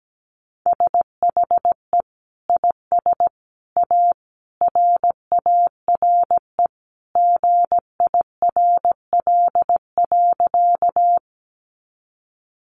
Morse code, heard as a keyed tone, about 17 wpm, 700 Hz. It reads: SHE IS A RARE GIRL.
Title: Sentences from Top 1100 Words 17wpm